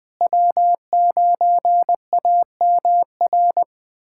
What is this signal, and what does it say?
Morse code, heard as a keyed tone, about 21 wpm, 700 Hz. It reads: W9AMR
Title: USA Callsign DE Callsign - 5 Characters - 20wpm